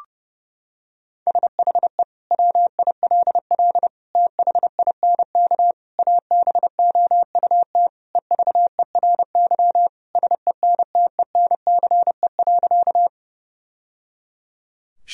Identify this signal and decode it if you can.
Morse code, heard as a keyed tone, about 30 wpm, 700 Hz. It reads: SHE WILL THINK ABOUT EVERY SENTENCE.
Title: Sentences from Top 1000 Words - Review Part 1 - 30wpm